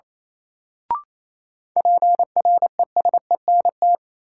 Morse code, heard as a keyed tone, about 28 wpm, 700 Hz. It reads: PRESENT
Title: Sentences from Top 800 Words - New Word Drill - 28wpm